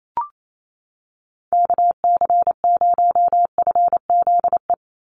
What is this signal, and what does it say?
Morse code, heard as a keyed tone, about 28 wpm, 700 Hz. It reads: KC0FZE
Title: USA Call Signs 28wpm